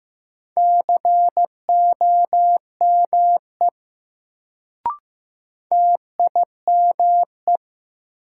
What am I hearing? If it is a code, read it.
Morse code, heard as a keyed tone, about 15 wpm, 700 Hz. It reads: COME TIME
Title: Sentences from Top 100 Words - New Word Drill - 15wpm